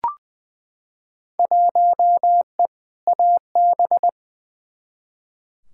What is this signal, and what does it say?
Morse code, heard as a keyed tone, about 20 wpm, 700 Hz. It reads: 1E AB
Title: Field Day Exchanges - 20wpm